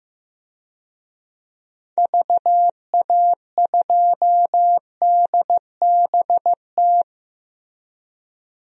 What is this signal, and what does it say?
Morse code, heard as a keyed tone, about 15 wpm, 700 Hz. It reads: VA2DBT
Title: DXCC-100 Call Signs - 6 Characters - Repeated 2x - 15wpm